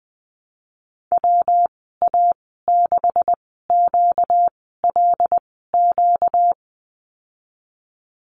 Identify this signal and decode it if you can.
Morse code, heard as a keyed tone, about 20 wpm, 700 Hz. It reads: W A 6 Q L Q
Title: 238 Lesson 38g - Letter q - Call Signs 20wpm